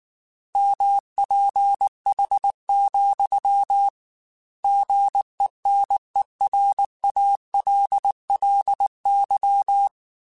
Morse code, heard as a keyed tone, about 19 wpm, 800 Hz. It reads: MPH, GENERALLY
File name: Weather Sets / Weather Sets-mp3s / Weather 5 12 @18